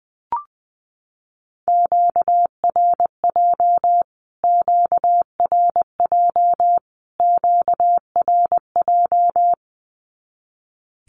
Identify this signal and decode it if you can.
Morse code, heard as a keyed tone, about 20 wpm, 700 Hz. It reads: QRJ QRJ QRJ
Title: Q-Codes Repeated 3x 20wpm